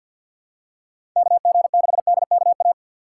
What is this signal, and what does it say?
Morse code, heard as a keyed tone, about 50 wpm, 700 Hz. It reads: KG6DKA
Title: USA Call Signs - 6 Characters - 4x Character Spacing - 50wpm